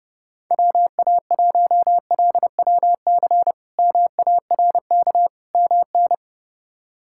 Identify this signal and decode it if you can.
Morse code, heard as a keyed tone, about 30 wpm, 700 Hz. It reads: WA1LWC MARK MN
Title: K1USN SST QSO Practice - 30wpm